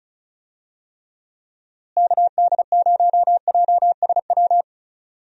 Morse code, heard as a keyed tone, about 34 wpm, 700 Hz. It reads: KD0JSW
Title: USA Call Signs - 6 Characters - Repeated 3x - 35wpm